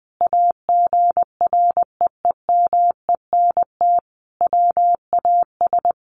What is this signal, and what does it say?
Morse code, heard as a keyed tone, about 19 wpm, 700 Hz. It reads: AGREEMENT WAS
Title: Sentences from Top 1500 Words 20wpm